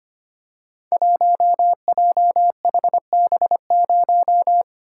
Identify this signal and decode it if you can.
Morse code, heard as a keyed tone, about 25 wpm, 700 Hz. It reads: 1JHB0
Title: Code Groups - 5 Letter:Numbers - 25wpm